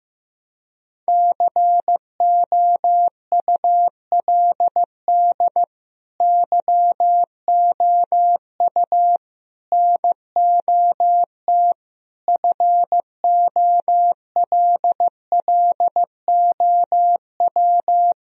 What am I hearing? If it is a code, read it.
Morse code, heard as a keyed tone, about 16 wpm, 700 Hz. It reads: COULD YOU NOT FOLLOW
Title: Sentences from Top 1000 Words - Review Part 1 - 15wpm